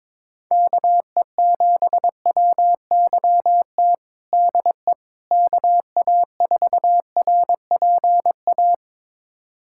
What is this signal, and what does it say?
Morse code, heard as a keyed tone, about 21 wpm, 700 Hz. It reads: KE7WYT DE KA4RPA
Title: USA Callsign DE Callsign 22wpm